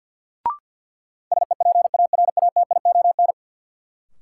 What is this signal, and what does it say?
Morse code, heard as a keyed tone, about 51 wpm, 700 Hz. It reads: SEPARATION
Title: Sets of 5 Words - New Word Drill - 50wpm